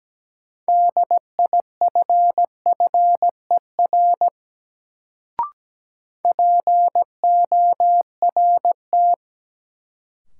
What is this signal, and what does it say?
Morse code, heard as a keyed tone, about 17 wpm, 700 Hz. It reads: DIFFER PORT
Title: Sentences from Top 1000 Words - Review Part 1 - New Word Drill - 17wpm